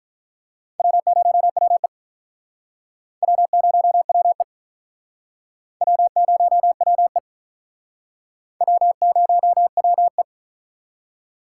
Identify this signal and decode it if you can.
Morse code, heard as a keyed tone, about 44 wpm, 700 Hz. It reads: W0WE W0WE W0WE W0WE